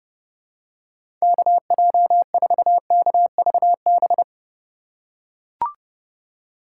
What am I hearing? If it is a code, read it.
Morse code, heard as a keyed tone, about 30 wpm, 700 Hz. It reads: KJ4KVB
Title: USA Call Signs - 2x Character Spacing - 30wpm